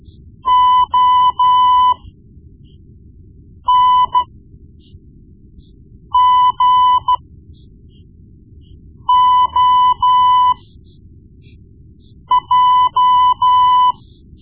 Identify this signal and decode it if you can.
Morse code, heard as a keyed tone, about 13 wpm, spaced as for 4 wpm, 1000 Hz. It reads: ONGOJ
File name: Morse Lessons / Morse Lessons / Tape-1B